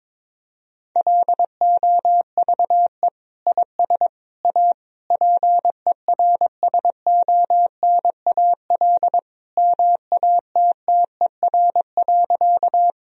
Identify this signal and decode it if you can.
Morse code, heard as a keyed tone, about 21 wpm, 700 Hz. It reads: LOVE IS A PERSONAL MATTER.
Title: Sentences from Top 1000 Words - Review Part 4 - 22wpm